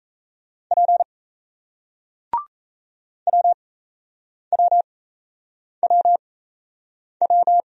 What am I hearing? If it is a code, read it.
Morse code, heard as a keyed tone, about 38 wpm, spaced as for 8 wpm, 700 Hz. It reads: P WWWW